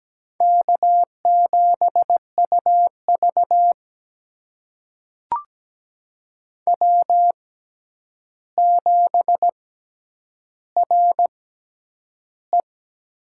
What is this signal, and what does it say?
Morse code, heard as a keyed tone, about 17 wpm, spaced as for 9 wpm, 700 Hz. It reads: K7UV W7RE
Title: USA Call Signs - 4 Characters - 6x Character Spacing - 17wpm